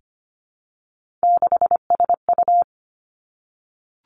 Morse code, heard as a keyed tone, about 25 wpm, 700 Hz. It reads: 6SU
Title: Three Challenging Letter-Number Combinations 25wpm